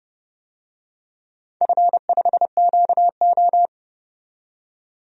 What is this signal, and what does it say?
Morse code, heard as a keyed tone, about 30 wpm, 700 Hz. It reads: F5QO
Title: DXCC-100 Call Signs - 4 Characters - 30wpm